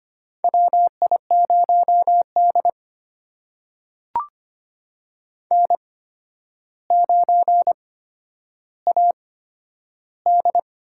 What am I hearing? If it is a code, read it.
Morse code, heard as a keyed tone, about 25 wpm, spaced as for 11 wpm, 700 Hz. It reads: WI0D N9AD
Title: USA Call Signs - 8x Character Spacing - 25wpm